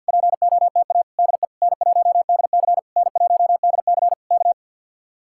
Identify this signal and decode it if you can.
Morse code, heard as a keyed tone, about 66 wpm, spaced as for 58 wpm, 700 Hz. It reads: POTA DE N1DC N1DC K
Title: Speed-Racing Common QSO Phrases - 50wpm